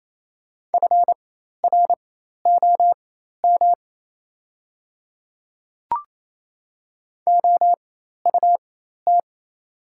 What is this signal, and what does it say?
Morse code, heard as a keyed tone, about 28 wpm, spaced as for 13 wpm, 700 Hz. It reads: FROM OUT
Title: 4x Character Spacing - Top 200 Words - 28wpm